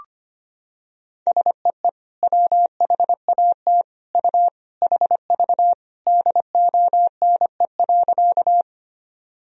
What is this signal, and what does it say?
Morse code, heard as a keyed tone, about 25 wpm, 700 Hz. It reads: SEE WHAT U HV DONE.
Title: Sentences from Top 500 Words and Abbreviations 25wpm